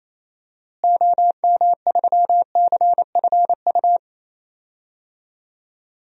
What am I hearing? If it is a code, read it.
Morse code, heard as a keyed tone, about 28 wpm, 700 Hz. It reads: OM3CFU
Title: DXCC-100 Call Signs - 6 Characters - Repeated 3x - 28wpm